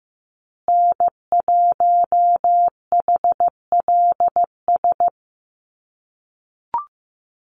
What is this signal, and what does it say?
Morse code, heard as a keyed tone, about 15 wpm, 700 Hz. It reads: N1HLS
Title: USA Call Signs Repeated 3x 15wpm